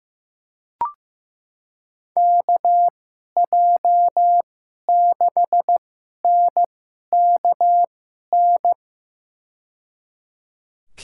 Morse code, heard as a keyed tone, about 15 wpm, spaced as for 11 wpm, 700 Hz. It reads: KJ6NKN
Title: USA Call Signs - 2x Character Spacing - 15wpm